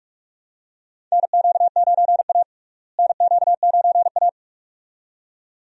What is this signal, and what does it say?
Morse code, heard as a keyed tone, about 46 wpm, 700 Hz. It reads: NQ9A NQ9A
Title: USA Call Signs - 4 Characters - Repeated 2x - 45wpm